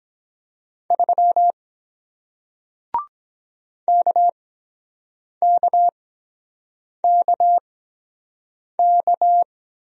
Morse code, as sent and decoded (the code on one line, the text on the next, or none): ...-- / -.- -.- -.- -.-
3 KKKK